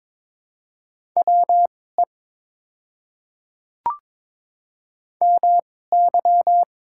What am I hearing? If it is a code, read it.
Morse code, heard as a keyed tone, about 22 wpm, spaced as for 16 wpm, 700 Hz. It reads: WE MY
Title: 2x Character Spacing - Top 200 Words - 22wpm